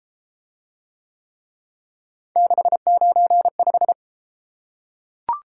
Morse code, heard as a keyed tone, about 31 wpm, 700 Hz. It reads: 695